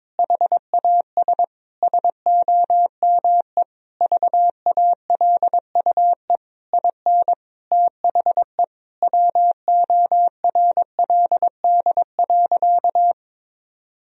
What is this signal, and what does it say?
Morse code, heard as a keyed tone, about 22 wpm, 700 Hz. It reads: HAS SOME VALUE IN THE WORLD.
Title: Sentences from Top 1000 Words - Review Part 3 - 22wpm